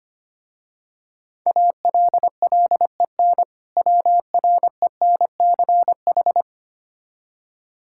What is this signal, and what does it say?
Morse code, heard as a keyed tone, about 25 wpm, 700 Hz. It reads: ALLEN WRENCH
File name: practice / 25wpm / Difficult Words with Hints - 25wpm